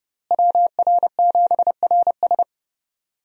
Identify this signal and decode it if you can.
Morse code, heard as a keyed tone, about 30 wpm, 700 Hz. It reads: WR7RS
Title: USA Call Signs - 5 Characters - Repeated 3x - 30wpm